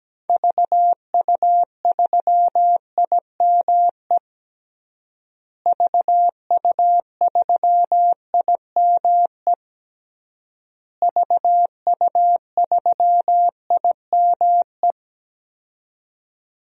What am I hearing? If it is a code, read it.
Morse code, heard as a keyed tone, about 17 wpm, 700 Hz. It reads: VU3IME VU3IME VU3IME
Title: DXCC-100 Call Signs - 6 Characters - Repeated 3x - 17wpm